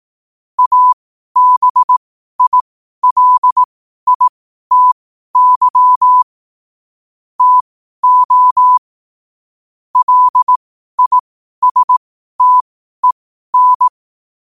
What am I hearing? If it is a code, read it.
Morse code, heard as a keyed tone, about 18 wpm, spaced as for 12 wpm, 1000 Hz. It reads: ABILITY TO LISTEN
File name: qotdfiles / 12wpm